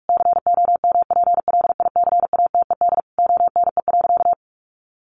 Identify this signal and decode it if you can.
Morse code, heard as a keyed tone, about 44 wpm, 700 Hz. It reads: COMPLICATED ONE.
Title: Sentences from Top 1200 Words 45wpm